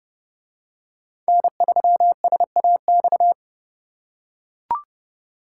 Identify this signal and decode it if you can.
Morse code, heard as a keyed tone, about 31 wpm, 700 Hz. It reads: N3SAX